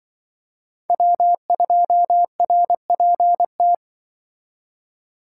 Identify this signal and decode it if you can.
Morse code, heard as a keyed tone, about 25 wpm, 700 Hz. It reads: W2RPT
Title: Speed Racing - USA 1x3 Call Signs 20wpm